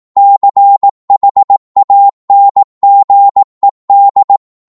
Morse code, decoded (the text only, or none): CHANGED